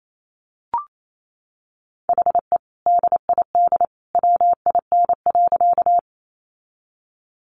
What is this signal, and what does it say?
Morse code, heard as a keyed tone, about 28 wpm, 700 Hz. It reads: HE DID WIN.
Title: Sentences from Top 1500 Words 28wpm